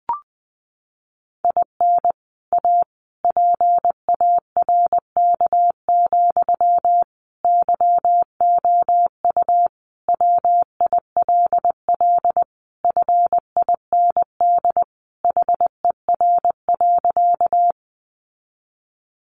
Morse code, heard as a keyed tone, about 20 wpm, 700 Hz. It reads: IN A PARK, YOU WILL FIND HER.
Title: Sentences from Top 1300 Words 20wpm